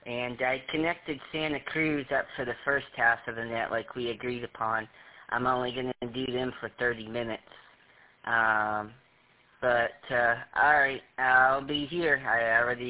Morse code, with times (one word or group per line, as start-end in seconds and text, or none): none